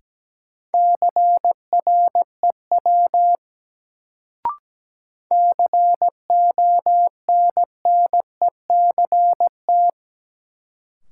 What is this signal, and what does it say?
Morse code, heard as a keyed tone, about 17 wpm, 700 Hz. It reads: CREW CONNECT
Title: Sentences from Top 1500 Words - New Word Drill - 17wpm